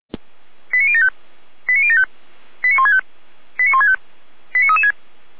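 Five-tone selective call (ZVEI), heard as quick runs of five tones, tones of about 70 ms. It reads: 89075 89075 89165 89165 89336
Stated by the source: ZVEI five-tone selective call received on SDR.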